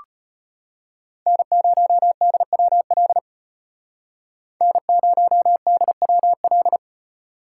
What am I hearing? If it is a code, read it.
Morse code, heard as a keyed tone, about 32 wpm, 700 Hz. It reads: N0DWL N0DWL